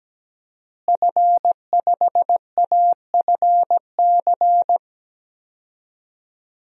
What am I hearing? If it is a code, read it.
Morse code, heard as a keyed tone, about 17 wpm, 700 Hz. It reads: F5AFC
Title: DXCC-100 Call Signs - 5 Characters - Repeated 3x - 17wpm